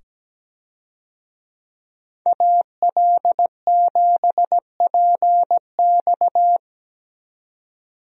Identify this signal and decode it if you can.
Morse code, heard as a keyed tone, about 17 wpm, 700 Hz. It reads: AL7PX
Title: USA Call Signs - 5 Characters - 17wpm